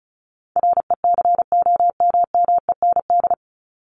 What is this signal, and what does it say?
Morse code, heard as a keyed tone, about 34 wpm, 700 Hz. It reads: RECOMMEND
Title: Top 1001-1250 Words 35wpm